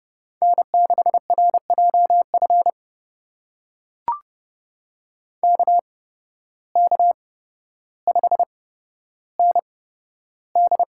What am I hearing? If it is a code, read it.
Morse code, heard as a keyed tone, about 30 wpm, spaced as for 13 wpm, 700 Hz. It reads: N6RJF KK5ND
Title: USA Call Signs - 5 Characters - 8x Character Spacing - 30wpm